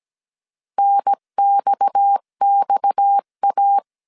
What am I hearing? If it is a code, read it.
Morse code, heard as a keyed tone, about 17 wpm, 800 Hz. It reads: NXXA
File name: CW / DET / 1 6 1500C S C 16wpm 2733